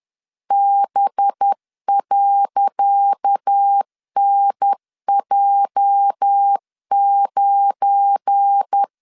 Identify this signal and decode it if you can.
Morse code, heard as a keyed tone, about 10 wpm, 800 Hz. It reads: B.NJ9
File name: LEZIONI / 10 SIGNS / 10D COMA 10wpm 3013